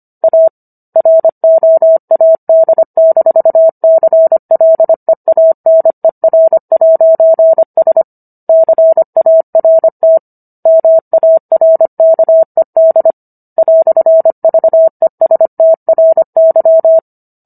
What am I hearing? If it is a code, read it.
Morse code, heard as a keyed tone, about 25 wpm, 650 Hz. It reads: A ROAD-CLEANER'S CART MARKED "VESTRY